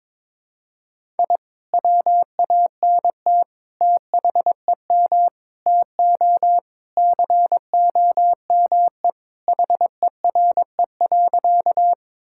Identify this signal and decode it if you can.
Morse code, heard as a keyed tone, about 22 wpm, 700 Hz. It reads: I WANT THEM TO COME HERE.
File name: practice / 22wpm / Sentences with 2-Word Phrases - 22wpm